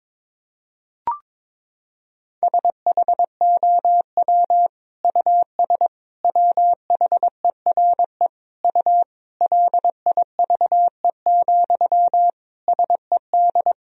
Morse code, heard as a keyed tone, about 22 wpm, 700 Hz. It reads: SHOW US WHERE U LIVE, SED
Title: Sentences from Top 200 Words and Abbreviations 22wpm